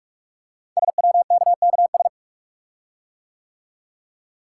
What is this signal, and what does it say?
Morse code, heard as a keyed tone, about 44 wpm, 700 Hz. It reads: SWKKS
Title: Code Groups - 5 Letters - 45wpm